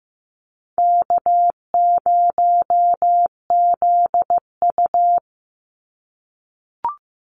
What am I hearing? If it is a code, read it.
Morse code, heard as a keyed tone, about 15 wpm, 700 Hz. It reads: K0ZU
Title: Easy Call Signs - 15wpm